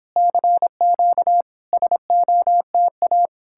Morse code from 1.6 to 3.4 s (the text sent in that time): SOTA